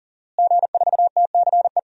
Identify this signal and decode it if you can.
Morse code, heard as a keyed tone, about 39 wpm, 700 Hz. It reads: G4TCE
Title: DXCC-100 Call Signs - 5 Characters - Repeated 3x - 40wpm